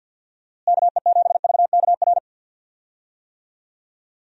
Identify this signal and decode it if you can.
Morse code, heard as a keyed tone, about 52 wpm, 700 Hz. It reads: KE7VKR
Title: USA Call Signs - 6 Characters - Repeated 3x - 50wpm